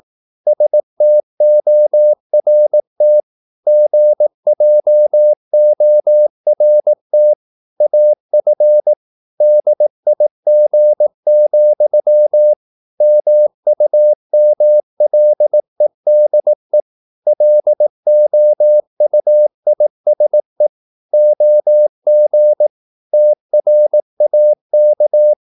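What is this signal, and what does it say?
Morse code, heard as a keyed tone, about 18 wpm, 600 Hz. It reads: STORT GJORT AF DIG, MUMLEDE LOUISE OG TRAK
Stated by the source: Slow (QRS) Morse practice at 18 wpm, CW med Gnister.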